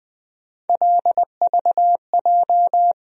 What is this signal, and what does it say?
Morse code, heard as a keyed tone, about 20 wpm, 700 Hz. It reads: LVJ
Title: USA Call Sign Postfixes - 3 Letters 20wpm